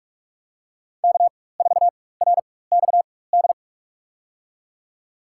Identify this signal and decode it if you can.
Morse code, heard as a keyed tone, about 41 wpm, spaced as for 20 wpm, 690 Hz. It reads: K4RXD
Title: USA Call Signs - 5 Characters - 4x Character Spacing - 45wpm